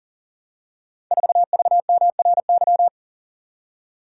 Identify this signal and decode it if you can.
Morse code, heard as a keyed tone, about 42 wpm, 700 Hz. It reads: 4VMRY
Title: Code Groups - 5 Letter:Numbers - 40wpm